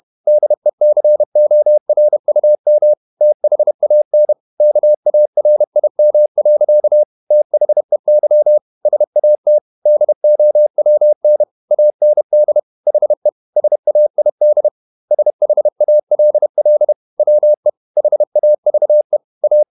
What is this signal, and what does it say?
Morse code, heard as a keyed tone, about 31 wpm, 600 Hz. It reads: DECORUM THAN KARIM. THEY SAT DOWN AND HE SAID SHALL WE HAVE A